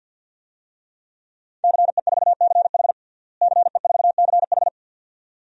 Morse code, heard as a keyed tone, about 51 wpm, 700 Hz. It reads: CE4CH CE4CH
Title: DXCC-100 Call Signs - 5 Characters - Repeated 2x - 50wpm